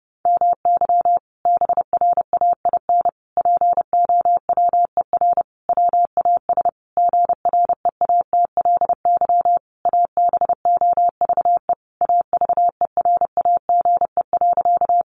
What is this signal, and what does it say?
Morse code, heard as a keyed tone, about 30 wpm, 700 Hz. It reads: MY BRAIN POWER WAS GREATLY ABOVE AVERAGE.
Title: Sentences from Top 1400 Words 30wpm